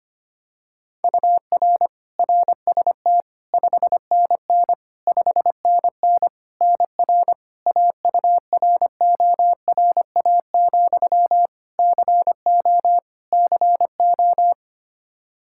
Morse code, heard as a keyed tone, about 25 wpm, 700 Hz. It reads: UR RST 5NN 5NN NR AURORA, CO CO
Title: Practice QSO - Exchange 1 - Beginning - 25wpm